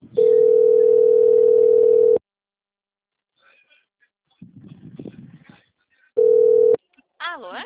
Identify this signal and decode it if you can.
Ringback tone (North America), 440 + 480 Hz.